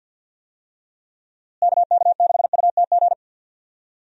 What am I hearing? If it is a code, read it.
Morse code, heard as a keyed tone, about 53 wpm, 700 Hz. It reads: KK6UTG
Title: USA Call Signs - 6 Characters - Repeated 3x - 50wpm